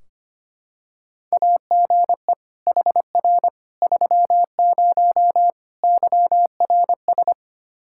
Morse code, heard as a keyed tone, about 25 wpm, 700 Hz. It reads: AGE HR 30 YRS
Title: Practice QSO - Exchange 3 - More Info - 25wpm